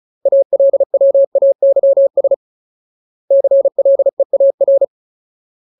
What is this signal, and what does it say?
Morse code, heard as a keyed tone, about 34 wpm, 550 Hz. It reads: ALWAYS CLEAR
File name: Ultimate Word List / 35WPM Ultimate